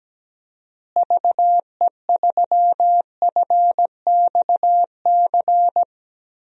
Morse code, heard as a keyed tone, about 17 wpm, 700 Hz. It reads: VE3FXC
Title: DXCC-100 Call Signs - 6 Characters - Repeated 3x - 17wpm